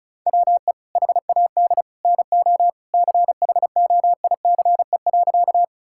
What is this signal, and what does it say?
Morse code, heard as a keyed tone, about 34 wpm, 700 Hz. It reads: WE HAD NO CHOICE.
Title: Sentences from Top 1000 Words - Review Part 6 - 35wpm